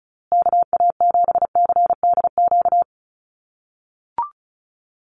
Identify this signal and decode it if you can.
Morse code, heard as a keyed tone, about 34 wpm, 700 Hz. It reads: KA7CDQ
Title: USA Call Signs Repeated 2x 35wpm